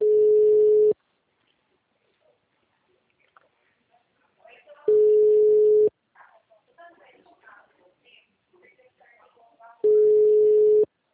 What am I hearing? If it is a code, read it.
Ringback tone (Europe, 425 Hz), with speech.